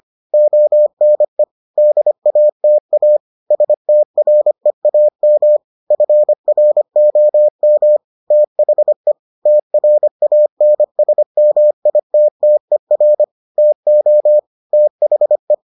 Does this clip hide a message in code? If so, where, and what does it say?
Yes: Morse code from 0.3 to 15.5 s, heard as a keyed tone, about 25 wpm, 600 Hz. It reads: ONE DATA STREAM FROM THE TRANSMITTER TO THE